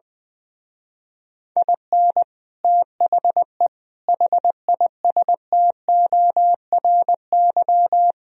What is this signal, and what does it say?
Morse code, heard as a keyed tone, about 20 wpm, 700 Hz. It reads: IN THE HISTORY
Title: Speed-Racing - Sets of 4 Words 20wpm